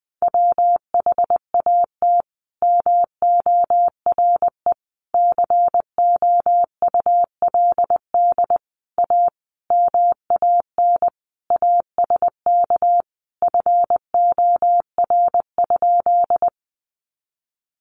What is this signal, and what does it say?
Morse code, heard as a keyed tone, about 20 wpm, 700 Hz. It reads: WHAT MORE COULD A MAN ASK FOR?
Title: Sentences from Top 200 Words 20wpm